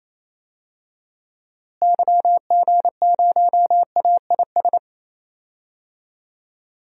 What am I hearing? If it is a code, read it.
Morse code, heard as a keyed tone, about 28 wpm, 700 Hz. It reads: YG0AIS
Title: DXCC-100 Call Signs - 6 Characters - Repeated 3x - 28wpm